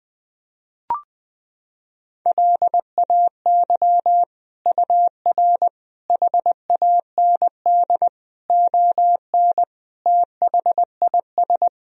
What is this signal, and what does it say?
Morse code, heard as a keyed tone, about 20 wpm, 700 Hz. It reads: LAY UR HAND ON THIS
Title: Sentences from Top 400 Words and Abbreviations 20wpm